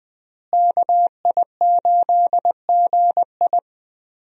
Morse code, heard as a keyed tone, about 20 wpm, 700 Hz. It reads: KI8GI
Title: USA Call Signs - 5 Characters - Repeated 2x - 20wpm